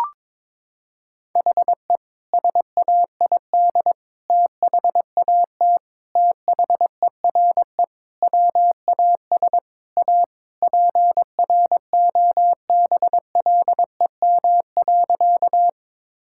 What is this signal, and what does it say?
Morse code, heard as a keyed tone, about 22 wpm, 700 Hz. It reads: HE SAID THAT THERE WAS A PROBLEM.
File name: practice / 22wpm / Sentences with 3-Word Phrases - 22wpm